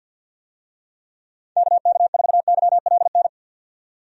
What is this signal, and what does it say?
Morse code, heard as a keyed tone, about 53 wpm, 700 Hz. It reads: KK4YLN
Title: USA Call Signs - 6 Characters - 50wpm